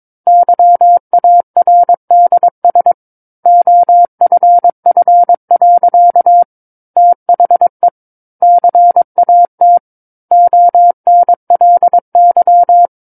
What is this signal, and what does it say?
Morse code, heard as a keyed tone, about 22 wpm, 700 Hz. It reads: YARDS OFF. THE CAT ONLY
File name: 06-Chapter-06 22 WPM